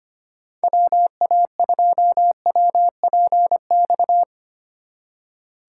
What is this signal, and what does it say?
Morse code, heard as a keyed tone, about 33 wpm, 700 Hz. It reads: WA2WPX